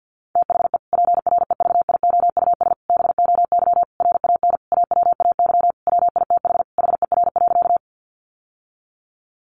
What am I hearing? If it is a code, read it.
Morse code, heard as a keyed tone, about 50 wpm, 700 Hz. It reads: THE PREVIOUS BOY RAN AWAY WITH HER.
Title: Sentences from Top 1500 Words 50wpm